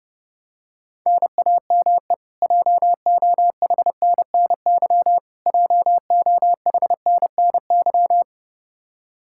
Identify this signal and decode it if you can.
Morse code, heard as a keyed tone, about 31 wpm, 700 Hz. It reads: NAME JOHNNY JOHNNY
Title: Practice QSO - Exchange 1 - Beginning - 30wpm